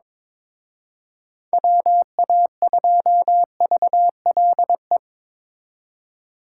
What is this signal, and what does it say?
Morse code, heard as a keyed tone, about 26 wpm, 700 Hz. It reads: WA2VLE
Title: Speed Racing - USA 2x3 Call Signs 22wpm